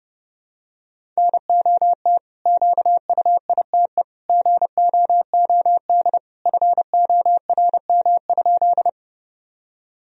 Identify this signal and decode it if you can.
Morse code, heard as a keyed tone, about 32 wpm, 700 Hz. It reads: NOT QUITE GOOD FORM?
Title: Sentences from Top 1000 Words - Review Part 1 - 30wpm